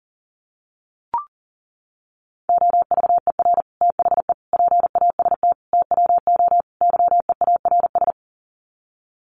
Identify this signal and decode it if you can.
Morse code, heard as a keyed tone, about 40 wpm, 700 Hz. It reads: OVER THE PAST TWO YEARS
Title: Sets of 5 Words Encore 40wpm